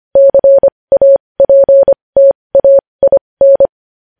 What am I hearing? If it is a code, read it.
Morse code, heard as a keyed tone, about 25 wpm, spaced as for 20 wpm, 550 Hz. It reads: CAPTAIN
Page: news-headlines-in-morse-code-at-20-wpm